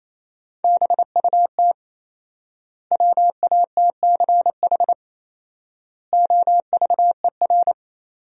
Morse code, heard as a keyed tone, about 28 wpm, 700 Hz. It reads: BUT WATCH OVER
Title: 4x Word-Spacing - Sentences from Top 300 Words - 28wpm